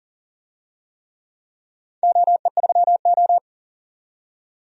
Morse code, heard as a keyed tone, about 40 wpm, 700 Hz. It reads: OE3O
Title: DXCC-100 Call Signs - 4 Characters - 40wpm